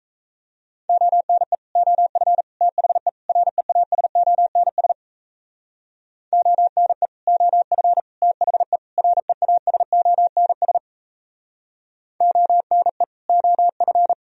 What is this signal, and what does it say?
Morse code, heard as a keyed tone, about 37 wpm, 700 Hz. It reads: ONE OF THE REASONS ONE OF THE REASONS ONE OF